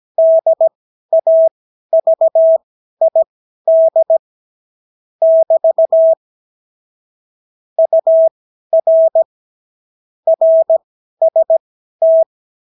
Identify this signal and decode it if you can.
Morse code, heard as a keyed tone, about 17 wpm, spaced as for 12 wpm, 650 Hz. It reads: DAVID <BT> UR RST
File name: CW QSO libroCW4 18-8ppm